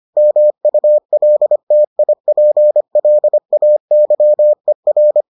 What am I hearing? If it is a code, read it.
Morse code, heard as a keyed tone, about 25 wpm, 600 Hz. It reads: MULTIPLAYER